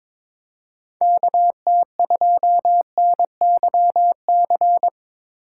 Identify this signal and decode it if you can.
Morse code, heard as a keyed tone, about 23 wpm, 700 Hz. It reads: KT2NYC
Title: USA Call Signs - 6 Characters - 3x Character Spacing - 22wpm